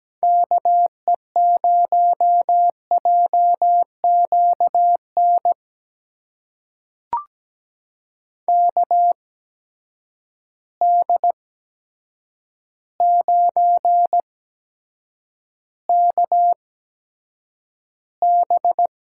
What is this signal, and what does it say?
Morse code, heard as a keyed tone, about 17 wpm, spaced as for 8 wpm, 700 Hz. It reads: KE0JQN KD9KB
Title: USA Call Signs - 6 Characters - 8x Character Spacing - 17wpm